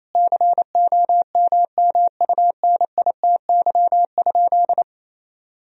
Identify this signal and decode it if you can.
Morse code, heard as a keyed tone, about 28 wpm, 700 Hz. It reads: COMMUNITY?
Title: Sentences from Top 1000 Words - Review Part 4 - 28wpm